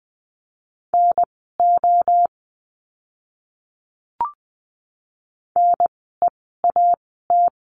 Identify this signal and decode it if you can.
Morse code, heard as a keyed tone, about 20 wpm, spaced as for 14 wpm, 700 Hz. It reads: NO NEAT
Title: 054 Lesson 10f - Letter r - Words 20wpm